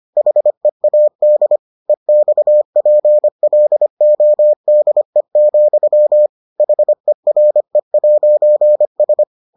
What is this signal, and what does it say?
Morse code, heard as a keyed tone, about 25 wpm, 600 Hz. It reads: HEAD EXPLODE, HERE'S